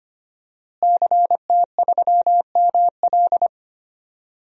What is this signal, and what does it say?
Morse code, heard as a keyed tone, about 25 wpm, 700 Hz. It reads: CT3ML
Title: DXCC-100 Call Signs - 5 Characters - Repeated 3x - 25wpm